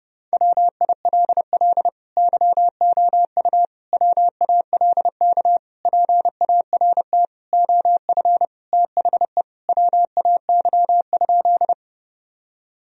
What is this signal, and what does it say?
Morse code, heard as a keyed tone, about 32 wpm, 700 Hz. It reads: WILL YOU WALK PART OF THE WAY?
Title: Sentences from Top 1000 Words - Review Part 1 - 30wpm